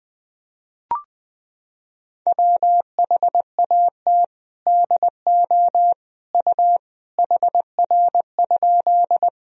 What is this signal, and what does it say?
Morse code, heard as a keyed tone, about 20 wpm, 700 Hz. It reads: WHAT DO U HR?
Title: Sentences from Top 300 Words and Abbreviations 20wpm